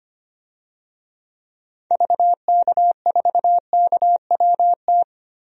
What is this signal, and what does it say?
Morse code, heard as a keyed tone, about 25 wpm, 700 Hz. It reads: VK4KWT
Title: DXCC-100 Call Signs - 6 Characters - Repeated 2x - 25wpm